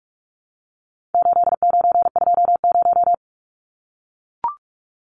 Morse code, heard as a keyed tone, about 44 wpm, 700 Hz. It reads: 8920
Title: Numbers - 4-Digit 45wpm